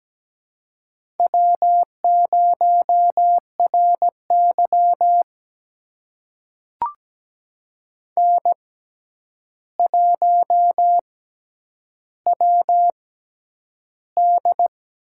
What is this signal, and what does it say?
Morse code, heard as a keyed tone, about 17 wpm, spaced as for 9 wpm, 700 Hz. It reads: W0RY N1WD
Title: USA Call Signs - 4 Characters - 6x Character Spacing - 17wpm